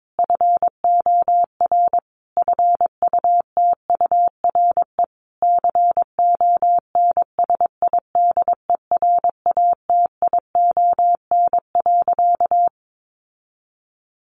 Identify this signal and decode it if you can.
Morse code, heard as a keyed tone, about 22 wpm, 700 Hz. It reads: FOR FUTURE CONSIDERATION.
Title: Sentences from Top 1500 Words 22wpm